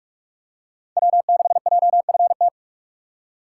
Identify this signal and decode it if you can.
Morse code, heard as a keyed tone, about 43 wpm, 700 Hz. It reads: W6JFT